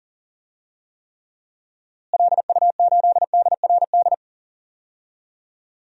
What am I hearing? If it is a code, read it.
Morse code, heard as a keyed tone, about 40 wpm, 700 Hz. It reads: LU8DRD
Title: DXCC-100 Call Signs - 6 Characters - Repeated 3x - 40wpm